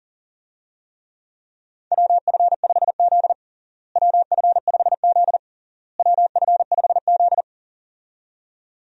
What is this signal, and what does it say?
Morse code, heard as a keyed tone, about 40 wpm, 700 Hz. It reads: WF5Z WF5Z WF5Z
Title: USA Call Signs - 4 Characters - Repeated 3x - 40wpm